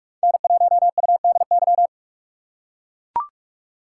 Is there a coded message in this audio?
Yes: Morse code from 0.2 to 1.9 s, heard as a keyed tone, about 40 wpm, 700 Hz. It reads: N1UDY